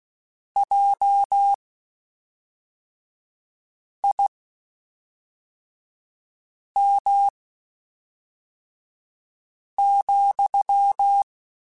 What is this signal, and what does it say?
Morse code, heard as a keyed tone, about 16 wpm, spaced as for 3 wpm, 800 Hz. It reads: JIM,
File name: QSOs / QSOs-mp3s / WD0BGP 1@5